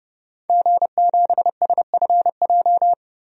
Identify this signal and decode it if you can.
Morse code, heard as a keyed tone, about 30 wpm, 700 Hz. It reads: G7SFJ
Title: DXCC-100 Call Signs - 5 Characters - 30wpm